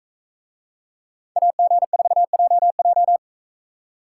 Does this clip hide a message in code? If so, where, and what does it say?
Yes: Morse code from 1.4 to 3.2 s, heard as a keyed tone, about 42 wpm, 700 Hz. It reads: AG4JJ